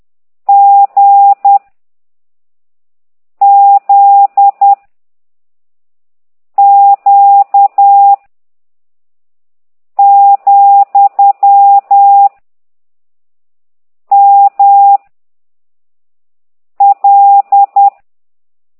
Morse code, heard as a keyed tone, about 10 wpm, 800 Hz. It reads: G Z Q , M L
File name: ct2idw / exerc13